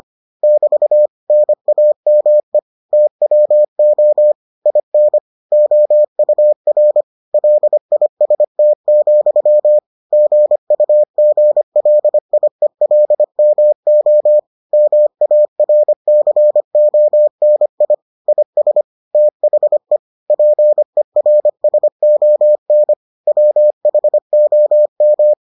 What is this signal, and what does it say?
Morse code, heard as a keyed tone, about 25 wpm, 600 Hz. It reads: <BT> NAME TWO IN OUR LIST, GUGLIELMO MARCONI IS THE PERSON WHOM